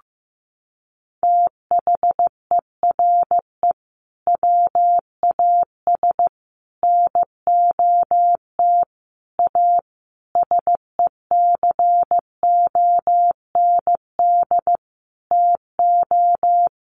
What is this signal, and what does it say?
Morse code, heard as a keyed tone, about 15 wpm, 700 Hz. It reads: THERE WAS NOT A SECOND TO
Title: Sentences from Top 1100 Words 15wpm